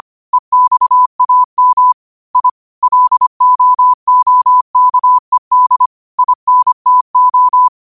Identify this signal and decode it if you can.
Morse code, heard as a keyed tone, about 25 wpm, 1000 Hz. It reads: EXAM I LOOKED INTO
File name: qotdfiles / 25wpm